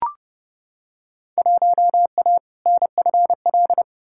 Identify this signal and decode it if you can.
Morse code, heard as a keyed tone, about 30 wpm, 700 Hz. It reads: 1A NFL
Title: Field Day Exchanges - 30wpm